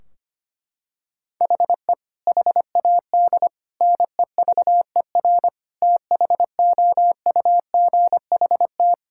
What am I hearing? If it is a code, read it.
Morse code, heard as a keyed tone, about 25 wpm, 700 Hz. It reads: HE HAD NEVER THOUGHT
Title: Sentences from Top 1600 Words 25wpm